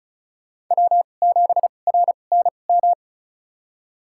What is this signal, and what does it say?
Morse code, heard as a keyed tone, about 36 wpm, spaced as for 25 wpm, 700 Hz. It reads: W7RNM
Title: USA Call Signs - 5 Characters - 2x Character Spacing - 35wpm